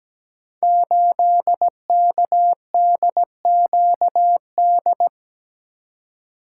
Morse code, heard as a keyed tone, about 17 wpm, 700 Hz. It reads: 8KDQD
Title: Code Groups - 5 Letter:Numbers - 17wpm